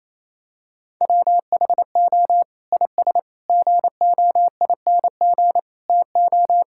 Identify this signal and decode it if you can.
Morse code, heard as a keyed tone, about 28 wpm, 700 Hz. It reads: WHO IS GOING TO
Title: Sentences from Top 1000 Words - Review Part 5 - 28wpm